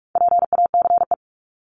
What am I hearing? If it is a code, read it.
Morse code, heard as a keyed tone, about 44 wpm, 700 Hz. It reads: PACE
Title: Top 1751-2000 Words 45wpm